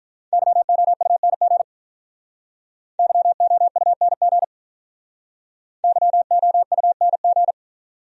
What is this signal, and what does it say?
Morse code, heard as a keyed tone, about 44 wpm, 700 Hz. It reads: YOUNG YOUNG YOUNG